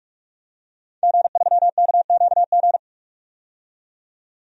Morse code, heard as a keyed tone, about 44 wpm, 700 Hz. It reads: G3KQG
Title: DXCC-100 Call Signs - 5 Characters - 45wpm